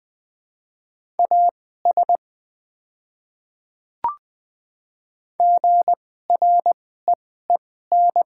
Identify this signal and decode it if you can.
Morse code, heard as a keyed tone, about 20 wpm, spaced as for 14 wpm, 700 Hz. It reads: AS GREEN
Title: 2x Character Spacing - Top 500 Words - 20wpm